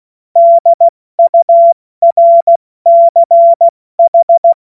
Morse code, heard as a keyed tone, about 16 wpm, 680 Hz. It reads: DURCH
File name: Morsekurs / Witze / Witz 10. Mai 2022 70 F80